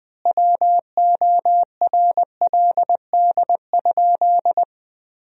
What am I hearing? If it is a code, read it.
Morse code, heard as a keyed tone, about 20 wpm, 700 Hz. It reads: WORLD?
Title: 4x Word-Spacing - Sentences from Top 200 Words - 20wpm